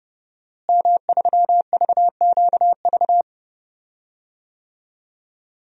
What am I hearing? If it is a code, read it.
Morse code, heard as a keyed tone, about 31 wpm, 700 Hz. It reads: M3VQV
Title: DXCC-100 Call Signs - 5 Characters - Repeated 3x - 30wpm